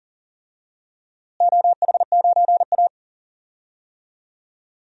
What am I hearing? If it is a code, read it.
Morse code, heard as a keyed tone, about 40 wpm, 700 Hz. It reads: OH9A
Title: DXCC-100 Call Signs - 4 Characters - Repeated 2x - 40wpm